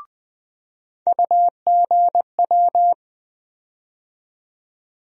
Morse code, heard as a keyed tone, about 20 wpm, 700 Hz. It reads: UGW
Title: USA Call Sign Postfixes - 3 Letters 20wpm